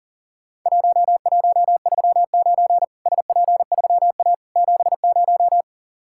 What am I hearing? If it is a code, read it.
Morse code, heard as a keyed tone, about 40 wpm, 700 Hz. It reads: 1139 SP3A 70
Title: Worked All Europe DX Contest - QTC Exchange - 40wpm